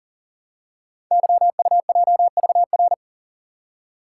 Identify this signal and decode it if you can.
Morse code, heard as a keyed tone, about 38 wpm, 700 Hz. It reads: YUJVR
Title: Code Groups - 5 Letters - 40wpm